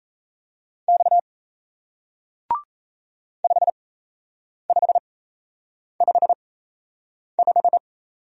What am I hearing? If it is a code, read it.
Morse code, heard as a keyed tone, about 38 wpm, spaced as for 8 wpm, 700 Hz. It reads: X 5555